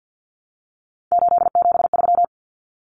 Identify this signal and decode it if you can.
Morse code, heard as a keyed tone, about 50 wpm, 700 Hz. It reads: 873
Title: Common Cut Numbers - 3-Digit 50wpm